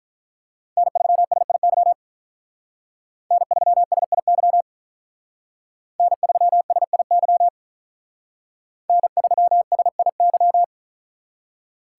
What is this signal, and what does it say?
Morse code, heard as a keyed tone, about 43 wpm, 700 Hz. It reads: N3SIY N3SIY N3SIY N3SIY
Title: Speed Racing - USA 1x3 Call Signs 35wpm